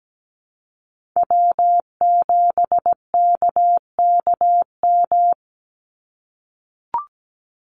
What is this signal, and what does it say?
Morse code, heard as a keyed tone, about 17 wpm, 700 Hz. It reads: W7KKM
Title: NA SOTA Chaser Callsigns 17wpm